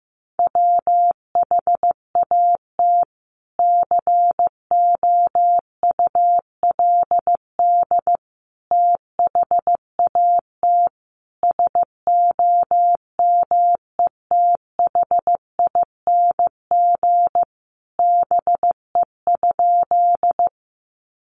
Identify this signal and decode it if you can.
Morse code, heard as a keyed tone, about 15 wpm, 700 Hz. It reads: WHAT COULD THAT SOMETHING BE?
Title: Sentences from Top 600 Words 15wpm